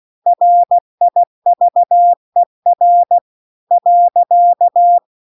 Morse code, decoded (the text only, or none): RIVER .